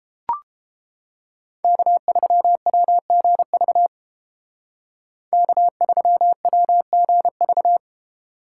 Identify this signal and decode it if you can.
Morse code, heard as a keyed tone, about 31 wpm, 700 Hz. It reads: K3WGV K3WGV